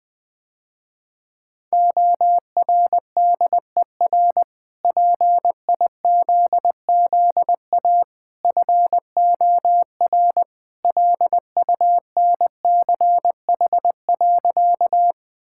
Morse code, heard as a keyed tone, about 20 wpm, 700 Hz. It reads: ORDER PIZZA FOR LUNCH.
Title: Easy Sentences with Hints - 20wpm